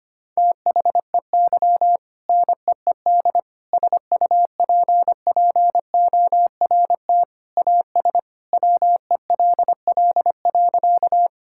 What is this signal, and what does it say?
Morse code, heard as a keyed tone, about 25 wpm, 700 Hz. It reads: THEY NEED SUPPORT AS WELL.
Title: Sentences from Top 1000 Words - Review Part 5 - 25wpm